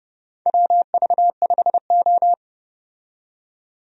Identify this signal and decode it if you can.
Morse code, heard as a keyed tone, about 31 wpm, 700 Hz. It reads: WV5O
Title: USA Call Signs - 4 Characters - 30wpm